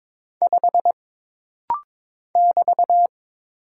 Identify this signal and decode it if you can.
Morse code, heard as a keyed tone, about 22 wpm, 700 Hz. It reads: 5 <BT>
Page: This rapid-fire timing is geared to push your ICR (Instant Character Recognition) skills.